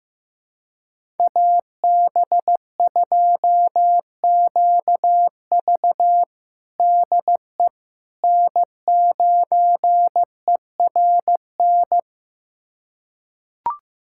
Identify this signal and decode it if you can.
Morse code, heard as a keyed tone, about 15 wpm, 700 Hz. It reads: AB2QV DE N9ERN
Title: USA Callsign DE Callsign - 5 Characters - 15wpm